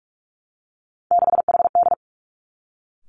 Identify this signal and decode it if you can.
Morse code, heard as a keyed tone, about 46 wpm, 700 Hz. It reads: 6HD
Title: Three Challenging Letter-Number Combinations 45wpm